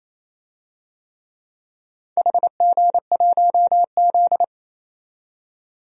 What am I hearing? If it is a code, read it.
Morse code, heard as a keyed tone, about 27 wpm, 700 Hz. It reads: HG1Z
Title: DXCC-100 Call Signs - 4 Characters - 28wpm